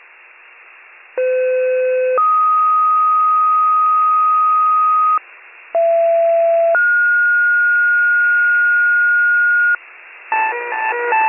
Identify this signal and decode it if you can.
Two-tone page, heard as a one-second tone, then a three-second tone, about 510 then 1250 Hz, about 670 then 1430 Hz, with noise. It is heard from 1.2 to 9.8 s.